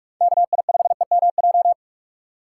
Morse code, heard as a keyed tone, about 46 wpm, 700 Hz. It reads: KI5EMJ
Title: USA Call Signs - 3x Character Spacing - 45wpm